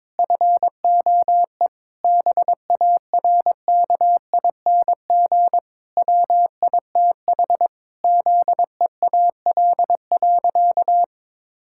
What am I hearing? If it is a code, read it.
Morse code, heard as a keyed tone, about 22 wpm, 700 Hz. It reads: FOE BARKING WITH ZEAL.
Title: Pangram Challenge v2 - 22wpm